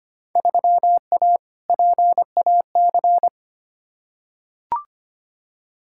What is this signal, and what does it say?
Morse code, heard as a keyed tone, about 25 wpm, 700 Hz. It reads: 3A PAC
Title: Field Day Exchanges Encore - 25wpm